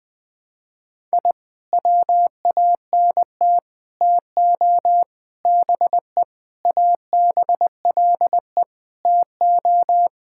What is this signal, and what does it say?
Morse code, heard as a keyed tone, about 20 wpm, 700 Hz. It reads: I WANT TO BE ABLE TO
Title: Sentences with 5-Word Phrases - Spoken Beginning - 20wpm